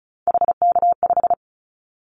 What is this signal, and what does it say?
Morse code, heard as a keyed tone, about 34 wpm, 700 Hz. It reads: HK5
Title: Callsign Prefixes for Top 100 DXCC - 35wpm